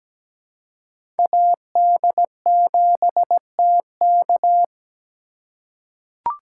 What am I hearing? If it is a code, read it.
Morse code, heard as a keyed tone, about 17 wpm, 700 Hz. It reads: AD7TK
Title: USA Call Signs - 4x Character Spacing - 17wpm